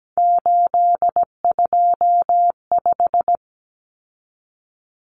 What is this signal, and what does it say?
Morse code, heard as a keyed tone, about 17 wpm, 700 Hz. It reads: 825
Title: Common Cut Numbers - 3-Digit 17wpm